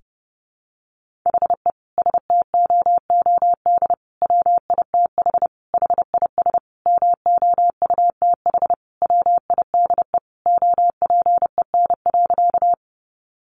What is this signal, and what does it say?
Morse code, heard as a keyed tone, about 30 wpm, 700 Hz. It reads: HE STOOD WITH HIS MOUTH WIDE OPEN.
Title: Sentences from Top 1500 Words 30wpm